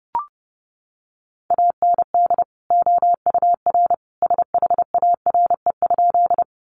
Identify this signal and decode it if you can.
Morse code, heard as a keyed tone, about 30 wpm, 700 Hz. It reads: AND OUR SHARE?
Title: Sentences from Top 900 Words 30wpm